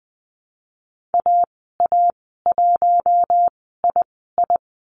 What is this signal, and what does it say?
Morse code, heard as a keyed tone, about 20 wpm, spaced as for 14 wpm, 700 Hz. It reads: AA1II
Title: 088 Lesson 15g - Number 5 - Call Signs 20wpm